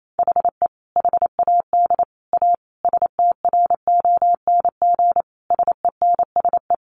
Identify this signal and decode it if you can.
Morse code, heard as a keyed tone, about 28 wpm, 700 Hz. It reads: HE HAD A STRONG SENSE
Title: Sentences from Top 1500 Words 28wpm